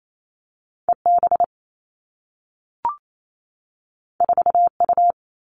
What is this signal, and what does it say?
Morse code, heard as a keyed tone, about 28 wpm, 700 Hz. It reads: E B 4 U
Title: Two Challenging Letter-Number Combinations 28wpm